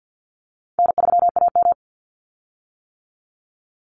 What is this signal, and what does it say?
Morse code, heard as a keyed tone, about 50 wpm, 700 Hz. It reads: N3AM
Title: Call Signs - 50wpm